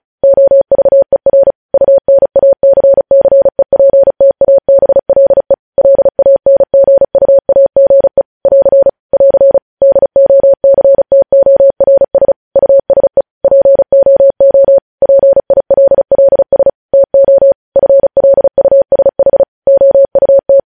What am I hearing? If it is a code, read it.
Morse code, heard as a keyed tone, about 36 wpm, 550 Hz. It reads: OVER UNACCEPTABLE LANGUAGE <AR> <AR> DOCTORS USE POO PILLS TO FLUSH OUT
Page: news headlines in morse code at 35 wpm